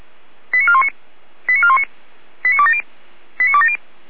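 Five-tone selective call (ZVEI), heard as quick runs of five tones, tones of about 80 ms. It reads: 89419 89419 89270 89270
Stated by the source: ZVEI five-tone selective call 89208, received on an SDR.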